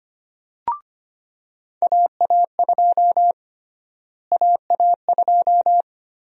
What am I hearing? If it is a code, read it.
Morse code, heard as a keyed tone, about 25 wpm, 700 Hz. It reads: AA2 AA2
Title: USA Call Sign Prefixes - Repeated 2x - 25wpm